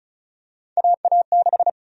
Morse code, heard as a keyed tone, about 34 wpm, 700 Hz. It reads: AA6
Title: USA Call Sign Prefixes - 35wpm